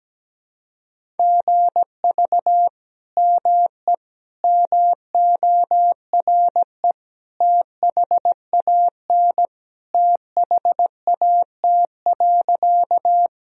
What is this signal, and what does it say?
Morse code, heard as a keyed tone, about 17 wpm, 700 Hz. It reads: GV ME MORE THAN THAT.
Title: Sentences from Top 200 Words and Abbreviations 17wpm